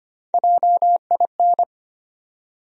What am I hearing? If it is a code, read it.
Morse code, heard as a keyed tone, about 25 wpm, 700 Hz. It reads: JIN
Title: USA Call Sign Postfixes - 3 Letters 25wpm